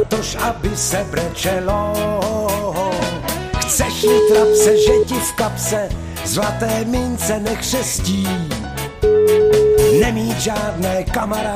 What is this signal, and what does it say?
Ringback tone (Europe, 425 Hz), with noise.